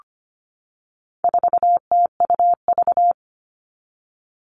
Morse code, heard as a keyed tone, about 25 wpm, 700 Hz. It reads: 4TUV
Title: All Cut Numbers - 4-Digit 25wpm